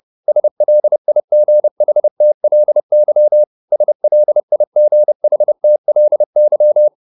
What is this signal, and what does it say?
Morse code, heard as a keyed tone, about 30 wpm, 600 Hz. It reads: SLIGHTLY SLIGHTLY